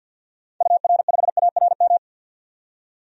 Morse code, heard as a keyed tone, about 52 wpm, 700 Hz. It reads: UR5ARM
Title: DXCC-100 Call Signs - 6 Characters - 50wpm